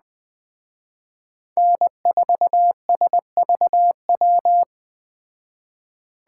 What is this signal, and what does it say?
Morse code, heard as a keyed tone, about 20 wpm, 700 Hz. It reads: N4SVW